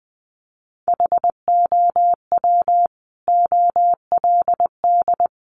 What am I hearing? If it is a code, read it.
Morse code, heard as a keyed tone, about 20 wpm, 700 Hz. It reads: HOW OLD
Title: Sentences from Top 200 Words 20wpm